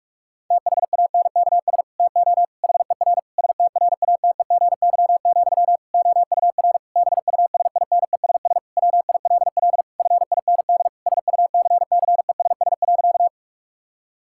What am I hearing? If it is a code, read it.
Morse code, heard as a keyed tone, about 46 wpm, 700 Hz. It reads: THANKS TO HER STRATEGY, OUR BUSINESS WILL FIND SUCCESS.
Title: Sentences from Top 1000 Words - Review Part 5 - 45wpm